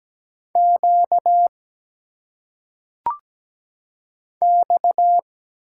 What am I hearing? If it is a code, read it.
Morse code, heard as a keyed tone, about 17 wpm, 700 Hz. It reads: Q X
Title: Single Letters - Phonetic Pronunciation 17wpm